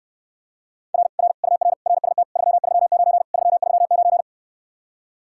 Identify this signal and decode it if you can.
Morse code, heard as a keyed tone, about 69 wpm, 700 Hz. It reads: R R UR RST 339 339